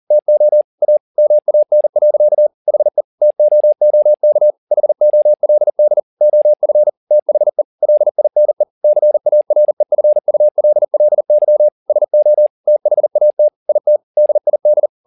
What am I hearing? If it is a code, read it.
Morse code, heard as a keyed tone, about 40 wpm, 600 Hz. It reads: TO A MAN. HE TOOK HOLD OF THE LINE CAREFULLY SO THAT IT DID
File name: the old man and the sea 2 40wpm